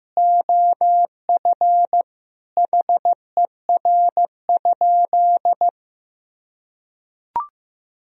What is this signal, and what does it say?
Morse code, heard as a keyed tone, about 15 wpm, 700 Hz. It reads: OF HER?
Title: Sentences from Top 200 Words and Abbreviations 15wpm